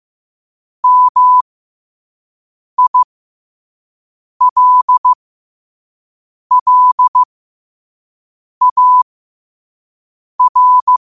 Morse code, heard as a keyed tone, about 15 wpm, spaced as for 5 wpm, 1000 Hz. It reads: MILLAR
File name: qotdfiles / 5wpm